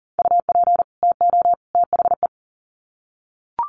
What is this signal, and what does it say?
Morse code, heard as a keyed tone, about 40 wpm, 700 Hz. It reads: UP TO THE
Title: Sets of 3 Words 40wpm